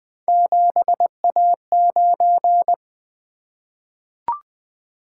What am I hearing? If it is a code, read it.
Morse code, heard as a keyed tone, about 20 wpm, 700 Hz. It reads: 7A9
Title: Three Letter-Number Combinations - 20wpm